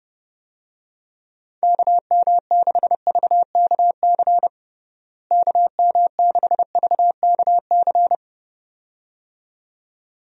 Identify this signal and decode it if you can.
Morse code, heard as a keyed tone, about 30 wpm, 700 Hz. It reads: KM6VKC KM6VKC
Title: USA Call Signs - 6 Characters - Repeated 2x - 30wpm